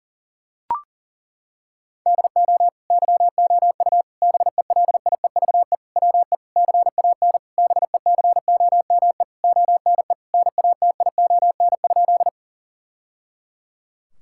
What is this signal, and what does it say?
Morse code, heard as a keyed tone, about 40 wpm, 700 Hz. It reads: DO YOU BELIEVE WE CAN BECOME ONE NATION?
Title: Sentences from Top 1000 Words - Review Part 6 - 40wpm